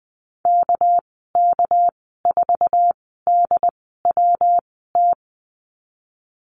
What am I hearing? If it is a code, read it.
Morse code, heard as a keyed tone, about 20 wpm, spaced as for 15 wpm, 700 Hz. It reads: KK4DWT
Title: 231 Lesson 37g - Letter x - Call Signs 20wpm